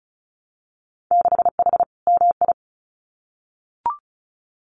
Morse code, heard as a keyed tone, about 34 wpm, 700 Hz. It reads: 6H MI
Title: Winter Field Day Exchanges Encore 35wpm